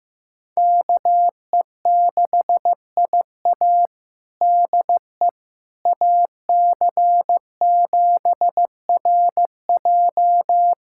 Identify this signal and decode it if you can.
Morse code, heard as a keyed tone, about 15 wpm, 700 Hz. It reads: KE6IA DE AC7RJ
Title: USA Callsign DE Callsign - 5 Characters - 15wpm